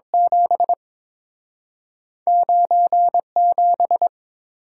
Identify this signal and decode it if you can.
Morse code, heard as a keyed tone, about 30 wpm, 700 Hz. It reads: 7 97